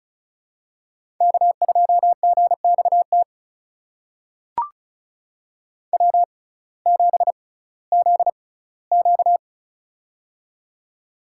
Morse code, heard as a keyed tone, about 36 wpm, spaced as for 19 wpm, 700 Hz. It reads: K2GXT W7ZQ
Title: USA Call Signs - 6x Character Spacing - 35wpm